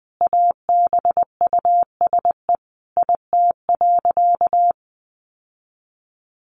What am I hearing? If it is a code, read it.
Morse code, heard as a keyed tone, about 20 wpm, 700 Hz. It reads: ABUSE IT.
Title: Sentences from Top 1300 Words 20wpm